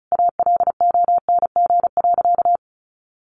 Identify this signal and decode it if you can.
Morse code, heard as a keyed tone, about 34 wpm, 700 Hz. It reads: ALONG.
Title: Sentences from Top 1500 Words 35wpm